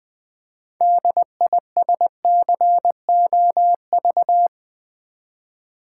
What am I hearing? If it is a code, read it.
Morse code, heard as a keyed tone, about 20 wpm, 700 Hz. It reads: DISCOV
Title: Build A Word - 3 Letters At A Time - 8-Letter Words - 20wpm